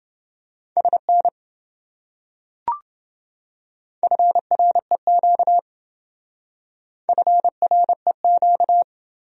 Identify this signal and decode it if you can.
Morse code, heard as a keyed tone, about 27 wpm, 700 Hz. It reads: SN FREQ FREQ